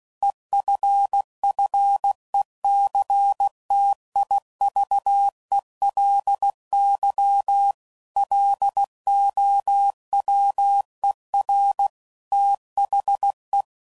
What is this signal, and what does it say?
Morse code, heard as a keyed tone, about 16 wpm, 800 Hz. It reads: EFFECTIVELY LOWER THE
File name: CW Code Groups @15WPM / Technical Subjects / XC